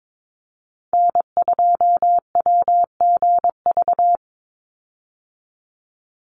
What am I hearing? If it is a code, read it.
Morse code, heard as a keyed tone, about 22 wpm, 700 Hz. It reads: N2WGV
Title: USA Call Signs 22wpm